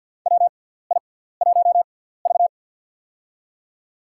Morse code, heard as a keyed tone, about 53 wpm, spaced as for 17 wpm, 700 Hz. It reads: WI1V
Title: USA Call Signs - 6x Character Spacing - 50wpm